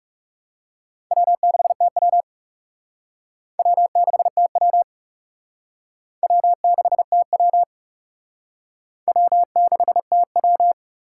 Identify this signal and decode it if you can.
Morse code, heard as a keyed tone, about 36 wpm, 700 Hz. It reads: W6TW W6TW W6TW W6TW
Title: Speed Racing - USA 1x2 Call Signs 30wpm